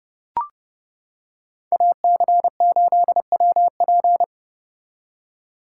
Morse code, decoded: AC8WP

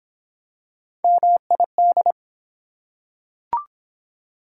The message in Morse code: MID

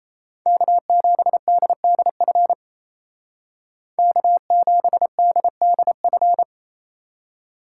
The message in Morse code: K7DDF K7DDF

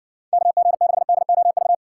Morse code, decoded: KG6DOV

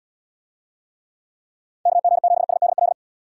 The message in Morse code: KG7SDL

